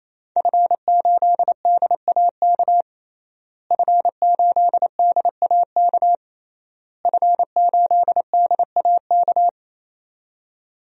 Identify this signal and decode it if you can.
Morse code, heard as a keyed tone, about 28 wpm, 700 Hz. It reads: F8DAK F8DAK F8DAK
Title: DXCC-100 Call Signs - 5 Characters - Repeated 3x - 28wpm